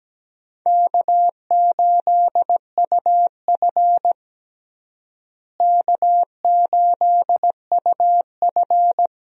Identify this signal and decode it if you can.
Morse code, heard as a keyed tone, about 17 wpm, 700 Hz. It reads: K8UF K8UF
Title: USA Call Signs - 4 Characters - Repeated 2x - 17wpm